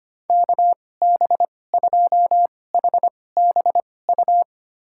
Morse code, heard as a keyed tone, about 25 wpm, spaced as for 18 wpm, 700 Hz. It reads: KB2HBU
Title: USA Call Signs - 6 Characters - 2x Character Spacing - 25wpm